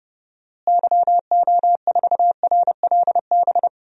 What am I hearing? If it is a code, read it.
Morse code, heard as a keyed tone, about 30 wpm, 700 Hz. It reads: YO4RLB
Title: DXCC-100 Call Signs - 6 Characters - Repeated 2x - 30wpm